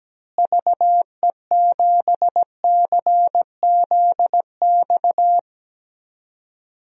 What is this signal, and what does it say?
Morse code, heard as a keyed tone, about 17 wpm, 700 Hz. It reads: VE7CZX
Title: DXCC-100 Call Signs - 6 Characters - Repeated 2x - 17wpm